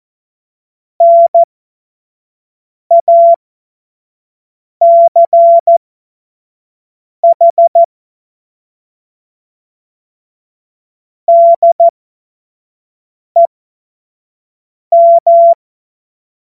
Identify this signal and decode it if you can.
Morse code, heard as a keyed tone, about 14 wpm, spaced as for 5 wpm, 680 Hz. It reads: NACH DEM